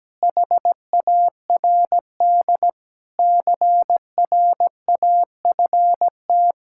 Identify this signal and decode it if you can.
Morse code, heard as a keyed tone, about 17 wpm, 700 Hz. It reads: HARD CRAFT
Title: Sentences from Top 1000 Words - Review Part 6 - 17wpm